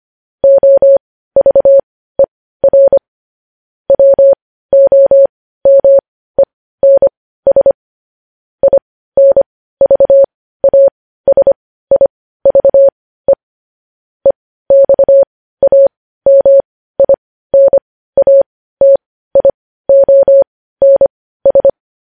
Morse code, heard as a keyed tone, about 25 wpm, spaced as for 15 wpm, 550 Hz. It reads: OVER WOMENS INVASIVE EXAMINATIONS
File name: podcast / News headlines-1x-25x15